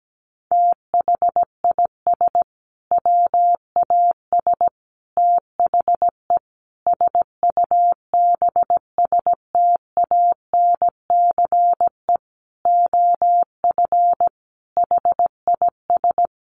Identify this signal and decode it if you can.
Morse code, heard as a keyed tone, about 17 wpm, 700 Hz. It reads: THIS WAS THE SUBSTANCE OF HIS
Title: Sentences from Top 1500 Words 17wpm